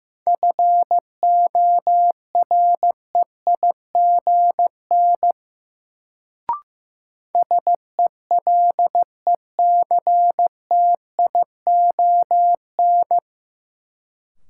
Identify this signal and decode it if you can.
Morse code, heard as a keyed tone, about 15 wpm, 700 Hz. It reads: FOREIGN SELECTION
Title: Sentences from Top 1400 Words - New Word Drill Alt - 15wpm